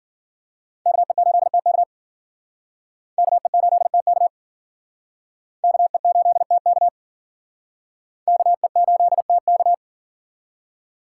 Morse code, heard as a keyed tone, about 54 wpm, 700 Hz. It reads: KE8TK KE8TK KE8TK KE8TK